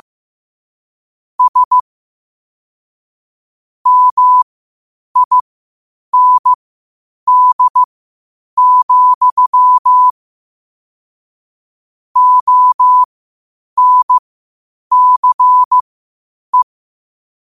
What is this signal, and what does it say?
Morse code, heard as a keyed tone, about 15 wpm, spaced as for 8 wpm, 1000 Hz. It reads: S MIND, ONCE
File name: qotdfiles / 8wpm